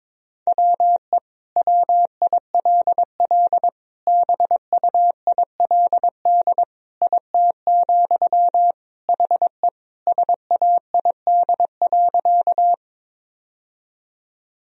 Morse code, heard as a keyed tone, about 22 wpm, 700 Hz. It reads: WE WILL BUILD IT, HE SAID.
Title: Sentences from Top 1000 Words - Review Part 1 - 22wpm